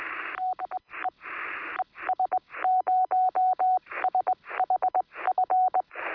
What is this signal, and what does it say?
Morse code, heard as a keyed tone, about 20 wpm, spaced as for 17 wpm, 740 Hz, with radio noise. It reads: DE ES0SHF
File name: sm7lcb / 2025z es0shf 23cm